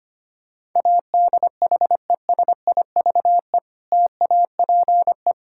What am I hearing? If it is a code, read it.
Morse code, heard as a keyed tone, about 25 wpm, 700 Hz. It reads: ADHESIVE TAPE
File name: practice / 25wpm / Difficult Words - Speed-Racing with Hints - 25wpm